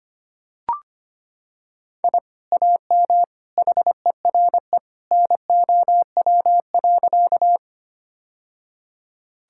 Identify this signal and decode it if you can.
Morse code, heard as a keyed tone, about 25 wpm, 700 Hz. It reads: I AM HERE NOW.
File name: practice / 25wpm / Sentences with 2-Word Phrases - 25wpm